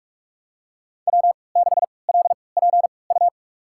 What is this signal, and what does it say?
Morse code, heard as a keyed tone, about 42 wpm, spaced as for 25 wpm, 700 Hz. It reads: W6LPU
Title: USA Call Signs - 5 Characters - 3x Character Spacing - 45wpm